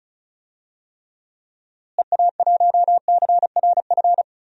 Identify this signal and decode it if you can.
Morse code, heard as a keyed tone, about 34 wpm, 700 Hz. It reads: EA1CRF
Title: DXCC-100 Call Signs - 6 Characters - Repeated 2x - 35wpm